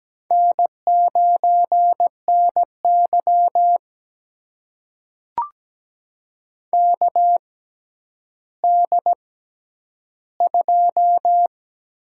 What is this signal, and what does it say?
Morse code, heard as a keyed tone, about 17 wpm, spaced as for 10 wpm, 700 Hz. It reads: N9NY KD2
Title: USA Call Signs - 4 Characters - 6x Character Spacing - 17wpm